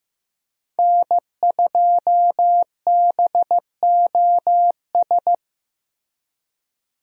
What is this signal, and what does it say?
Morse code, heard as a keyed tone, about 15 wpm, 700 Hz. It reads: N2BOS
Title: USA Call Signs - 5 Characters - Repeated 3x - 15wpm